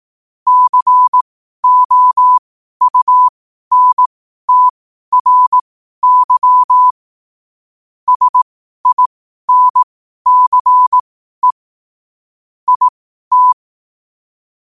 Morse code, heard as a keyed tone, about 18 wpm, spaced as for 12 wpm, 1000 Hz. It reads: COUNTRY SINCE IT
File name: qotdfiles / 12wpm